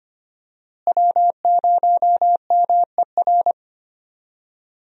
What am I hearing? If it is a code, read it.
Morse code, heard as a keyed tone, about 25 wpm, 700 Hz. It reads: W0MER
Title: USA Call Signs - 5 Characters - Repeated 3x - 25wpm